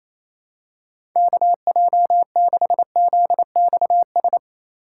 Morse code, heard as a keyed tone, about 28 wpm, 700 Hz. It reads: KJ6ZXS
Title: USA Call Signs - 6 Characters - 4x Character Spacing - 28wpm